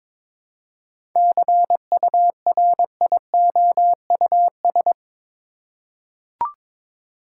Morse code, heard as a keyed tone, about 22 wpm, 700 Hz. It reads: CURIOUS
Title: Seven Letter Words - 22wpm